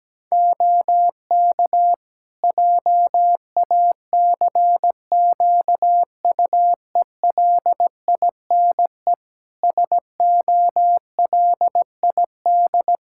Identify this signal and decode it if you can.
Morse code, heard as a keyed tone, about 17 wpm, 700 Hz. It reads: OK JACQUELINE SOLID
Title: K1USN SST QSO Practice 17wpm